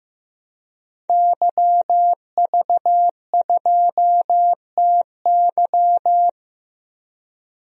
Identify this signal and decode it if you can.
Morse code, heard as a keyed tone, about 15 wpm, 700 Hz. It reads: YV2TY
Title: DXCC-100 Call Signs - 5 Characters - Repeated 3x - 15wpm